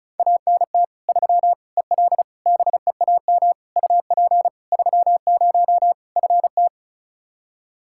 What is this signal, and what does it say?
Morse code, heard as a keyed tone, about 34 wpm, 700 Hz. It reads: ANT 3 EL BEAM UP 30 FT
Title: Practice QSO - Exchange 2 - Info - 35wpm